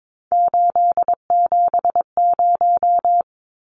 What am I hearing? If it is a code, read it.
Morse code, heard as a keyed tone, about 22 wpm, 700 Hz. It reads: 870
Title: Numbers - 3-Digit 22wpm